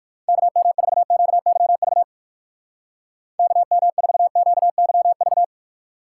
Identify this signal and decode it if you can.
Morse code, heard as a keyed tone, about 55 wpm, 700 Hz. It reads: KM4QYV KM4QYV
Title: Speed Racing - USA 2x3 Call Signs 45wpm